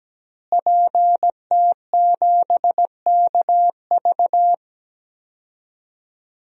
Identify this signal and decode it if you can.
Morse code, heard as a keyed tone, about 17 wpm, 700 Hz. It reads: PT7KV
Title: DXCC-100 Call Signs - 5 Characters - Repeated 3x - 17wpm